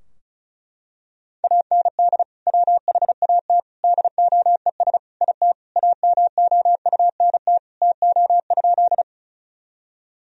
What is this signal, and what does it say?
Morse code, heard as a keyed tone, about 34 wpm, 700 Hz. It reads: AND WHAT DOES IT AMOUNT TO?
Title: Sentences from Top 1000 Words - Review Part 4 - 35wpm